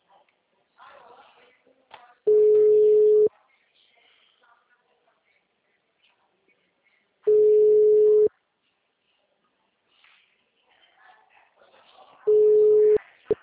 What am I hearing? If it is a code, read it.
Ringback tone (Europe, 425 Hz), with speech.